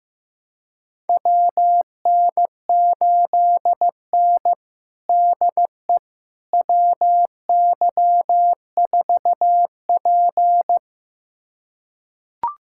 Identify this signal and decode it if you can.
Morse code, heard as a keyed tone, about 15 wpm, 700 Hz. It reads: WN8N DE WY4P
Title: USA Callsign DE Callsign - 4 Characters - 15wpm